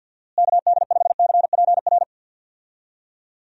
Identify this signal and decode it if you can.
Morse code, heard as a keyed tone, about 51 wpm, 700 Hz. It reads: KD5CPR
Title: USA Call Signs - 6 Characters - Repeated 2x - 50wpm